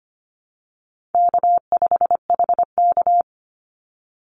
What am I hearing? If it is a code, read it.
Morse code, heard as a keyed tone, about 25 wpm, 700 Hz. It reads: K5HK
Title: Easy Call Signs Repeated 2x - 25wpm